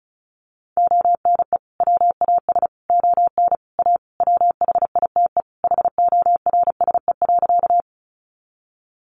Morse code, heard as a keyed tone, about 34 wpm, 700 Hz. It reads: ONE WAS ON A WHITE HORSE.
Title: Sentences from Top 300 Words 35wpm